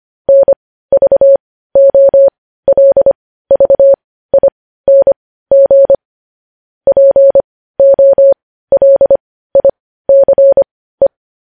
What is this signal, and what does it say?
Morse code, heard as a keyed tone, about 25 wpm, spaced as for 15 wpm, 550 Hz. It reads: NVOLVING POLICE
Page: news-headlines-in-morse-code-at-15-wpm